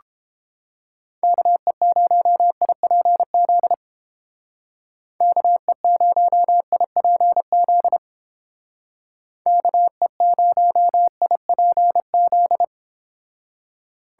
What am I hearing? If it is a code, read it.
Morse code, heard as a keyed tone, about 29 wpm, 700 Hz. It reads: KE0IPZ KE0IPZ KE0IPZ